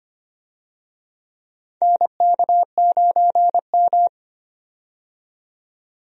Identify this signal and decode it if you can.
Morse code, heard as a keyed tone, about 25 wpm, 700 Hz. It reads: NK9M
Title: USA Call Signs - 4 Characters - 25wpm